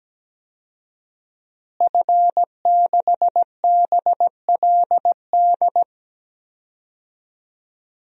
Morse code, heard as a keyed tone, about 17 wpm, 700 Hz. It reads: F6BLD
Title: DXCC-100 Call Signs - 5 Characters - Repeated 2x - 17wpm